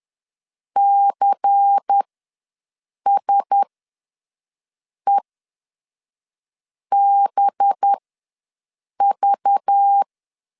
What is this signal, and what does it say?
Morse code, heard as a keyed tone, about 10 wpm, spaced as for 5 wpm, 800 Hz. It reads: CSEBV
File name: LEZIONI / 6 PKX0 / 6 PK ws 10wpm 3745